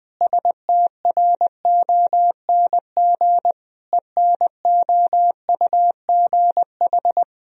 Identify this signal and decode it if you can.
Morse code, heard as a keyed tone, about 20 wpm, 700 Hz. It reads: STRONG ENOUGH
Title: Sentences from Top 1000 Words - Review Part 3 - 20wpm